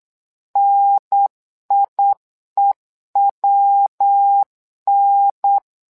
Morse code, heard as a keyed tone, about 9 wpm, 800 Hz. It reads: NIEWN
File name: LEZIONI / 3 MWRO / 3 MWRO 8wpm 2606